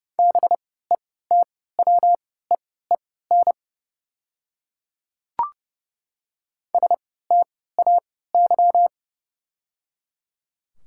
Morse code, heard as a keyed tone, about 30 wpm, spaced as for 17 wpm, 700 Hz. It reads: BETWEEN STAY
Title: 3x Character Spacing - Top 500 Words - 30wpm